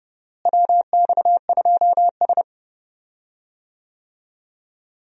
Morse code, heard as a keyed tone, about 32 wpm, 700 Hz. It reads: WX2S
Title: USA Call Signs - 4 Characters - Repeated 3x - 30wpm